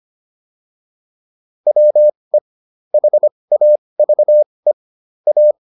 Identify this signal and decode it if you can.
Morse code, heard as a keyed tone, about 25 wpm, spaced as for 20 wpm, 600 Hz. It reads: WE HAVE A
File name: WD202 20